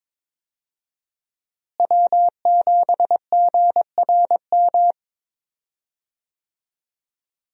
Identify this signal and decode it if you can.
Morse code, heard as a keyed tone, about 22 wpm, 700 Hz. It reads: W7GRM
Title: USA Call Signs - 5 Characters - Repeated 2x - 22wpm